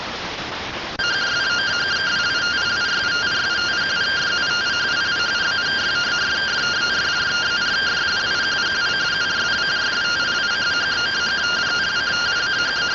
Two-tone teleprinter signal, about 170 Hz shift.